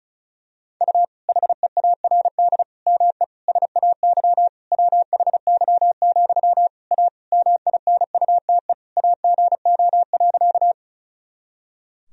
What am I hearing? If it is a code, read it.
Morse code, heard as a keyed tone, about 34 wpm, 700 Hz. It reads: U HEARD ME SAY WHY, A MINUTE AGO.
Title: Sentences from Top 500 Words and Abbreviations 35wpm